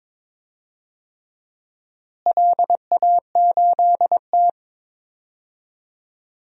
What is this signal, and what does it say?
Morse code, heard as a keyed tone, about 22 wpm, 700 Hz. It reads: LA8T
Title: DXCC-100 Call Signs - 4 Characters - Repeated 3x - 22wpm